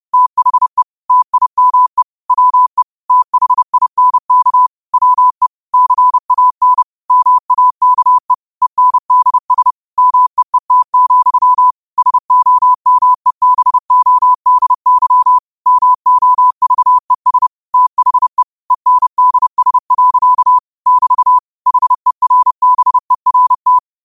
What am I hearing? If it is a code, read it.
Morse code, heard as a keyed tone, about 30 wpm, 1000 Hz. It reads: THE TIME WE THINK WE CAN MAKE ENDS MEET, SOMEBODY MOVES THE ENDS. <BT> HERBERT
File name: qotdfiles / 30wpm